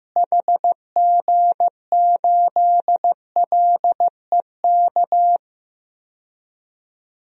Morse code, heard as a keyed tone, about 15 wpm, 700 Hz. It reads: HG8LEK
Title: DXCC-100 Call Signs - 6 Characters - Repeated 2x - 15wpm